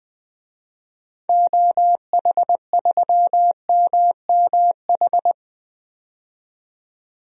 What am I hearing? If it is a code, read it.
Morse code, heard as a keyed tone, about 20 wpm, 700 Hz. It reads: OH3MMH
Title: DXCC-100 Call Signs - 6 Characters - 20wpm